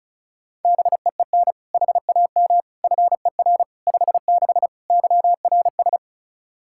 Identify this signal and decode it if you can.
Morse code, heard as a keyed tone, about 34 wpm, 700 Hz. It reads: BEEN HAM FER 56 YRS
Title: Practice QSO - Exchange 3 - More Info - 35wpm